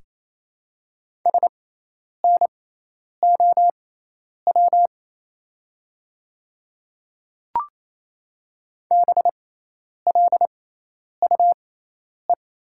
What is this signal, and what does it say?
Morse code, heard as a keyed tone, about 28 wpm, spaced as for 10 wpm, 700 Hz. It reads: SNOW BLUE
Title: 6x Character Spacing - Top 500 Words - 28wpm